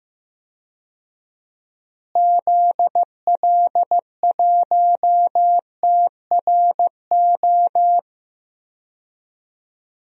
Morse code, heard as a keyed tone, about 15 wpm, 700 Hz. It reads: ZL1TRO
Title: DXCC-100 Call Signs - 6 Characters - Repeated 2x - 15wpm